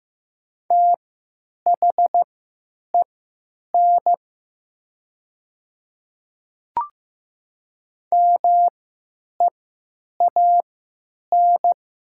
Morse code, heard as a keyed tone, about 15 wpm, spaced as for 9 wpm, 700 Hz. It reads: THEN MEAN
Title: 3x Character Spacing - Top 300 Words - 15wpm